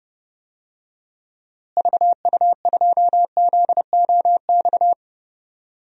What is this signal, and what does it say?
Morse code, heard as a keyed tone, about 30 wpm, 700 Hz. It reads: VU2ZOX
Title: DXCC-100 Call Signs - 6 Characters - Repeated 2x - 30wpm